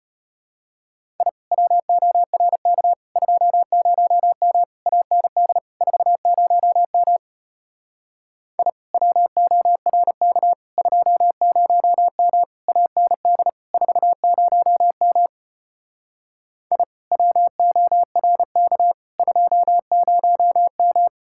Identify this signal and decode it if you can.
Morse code, heard as a keyed tone, about 33 wpm, 700 Hz. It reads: I WORK 20M AND 40M I WORK 20M AND 40M I WORK 20M